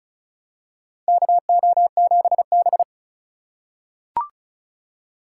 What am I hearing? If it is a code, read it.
Morse code, heard as a keyed tone, about 36 wpm, 700 Hz. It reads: KO7B
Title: USA Call Signs - 2x Character Spacing - 35wpm